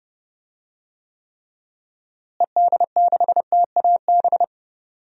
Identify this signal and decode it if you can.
Morse code, heard as a keyed tone, about 28 wpm, 700 Hz. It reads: ED6TAB
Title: DXCC-100 Call Signs - 6 Characters - Repeated 3x - 30wpm